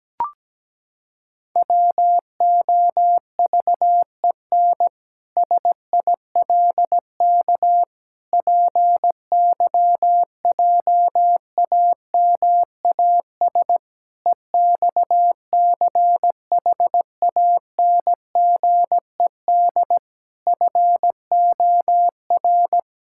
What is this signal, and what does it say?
Morse code, heard as a keyed tone, about 17 wpm, 700 Hz. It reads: WOVEN SILK PYJAMAS EXCHANGED FOR
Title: Pangram Challenge v2 - 17wpm